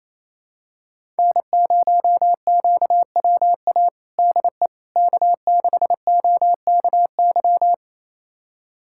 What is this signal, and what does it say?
Morse code, heard as a keyed tone, about 27 wpm, 700 Hz. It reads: N0QWA DE K6OKY
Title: USA Callsign DE Callsign - 5 Characters - 28wpm